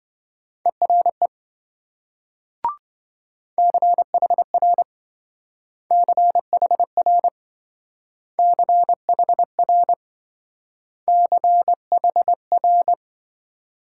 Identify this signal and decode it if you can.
Morse code, heard as a keyed tone, about 27 wpm, 700 Hz. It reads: ERE CHR CHR CHR CHR